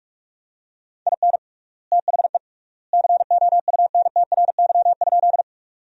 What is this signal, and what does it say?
Morse code, heard as a keyed tone, about 42 wpm, 700 Hz. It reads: IN THE COUNTRY?
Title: 3x Word-Spacing - Sentences from Top 300 Words - 45wpm